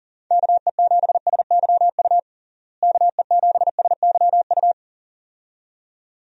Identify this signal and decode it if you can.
Morse code, heard as a keyed tone, about 37 wpm, 700 Hz. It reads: KE7SYU KE7SYU
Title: USA Call Signs - 6 Characters - Repeated 2x - 40wpm